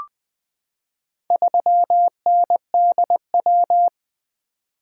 Morse code, heard as a keyed tone, about 20 wpm, 700 Hz. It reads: 3NDW
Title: Four Letter-Number Combinations - 20wpm